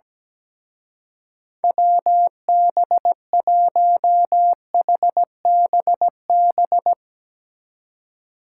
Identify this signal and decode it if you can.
Morse code, heard as a keyed tone, about 19 wpm, 700 Hz. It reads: WB1HBB
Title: Speed Racing - USA 2x3 Call Signs 17wpm